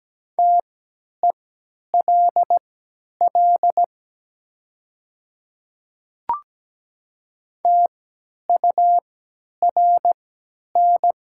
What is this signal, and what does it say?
Morse code, heard as a keyed tone, about 17 wpm, spaced as for 10 wpm, 700 Hz. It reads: TELL TURN
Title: 3x Character Spacing - Top 200 Words - 17wpm